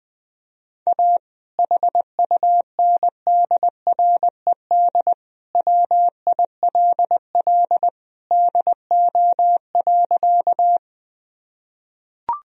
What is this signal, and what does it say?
Morse code, heard as a keyed tone, about 21 wpm, 700 Hz. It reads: A HUNDRED WILL DO.
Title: Sentences from Top 1000 Words - Review Part 2 - 20wpm